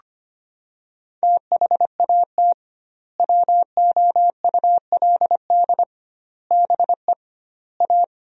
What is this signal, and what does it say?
Morse code, heard as a keyed tone, about 25 wpm, 700 Hz. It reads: THAT WOULD BE A
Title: 2x Word-Spacing - Sentences from Top 500 Words - 25wpm